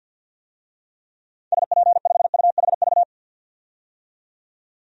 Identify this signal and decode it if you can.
Morse code, heard as a keyed tone, about 52 wpm, 700 Hz. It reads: SP5UHV
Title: DXCC-100 Call Signs - 6 Characters - 50wpm